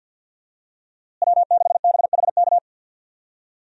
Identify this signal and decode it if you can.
Morse code, heard as a keyed tone, about 50 wpm, 700 Hz. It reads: W6BHK
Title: USA Call Signs - 5 Characters - Repeated 2x - 50wpm